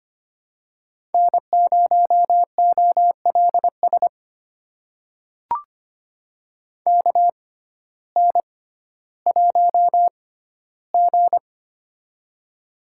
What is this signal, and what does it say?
Morse code, heard as a keyed tone, about 25 wpm, spaced as for 14 wpm, 700 Hz. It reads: N0OLS KN1G
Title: USA Call Signs - 6x Character Spacing - 25wpm